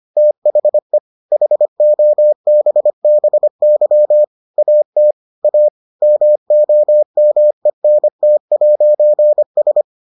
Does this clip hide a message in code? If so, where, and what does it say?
Yes: Morse code from 0.2 to 9.8 s, heard as a keyed tone, about 25 wpm, 600 Hz. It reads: THE HOBBY AT A MOMENT'S